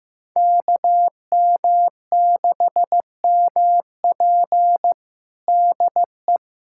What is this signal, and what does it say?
Morse code, heard as a keyed tone, about 15 wpm, 700 Hz. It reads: KM6MP DE
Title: USA Callsign DE Callsign - 5 Characters - 15wpm